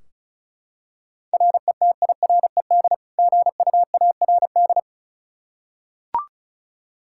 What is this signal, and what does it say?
Morse code, heard as a keyed tone, about 34 wpm, 700 Hz. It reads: RETIRED GUARD
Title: Practice QSO - Exchange 3 - More Info - 35wpm